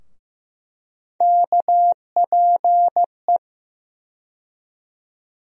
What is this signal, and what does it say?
Morse code, heard as a keyed tone, about 15 wpm, 700 Hz. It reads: KPE
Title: USA Call Sign Postfixes - 3 Letters 15wpm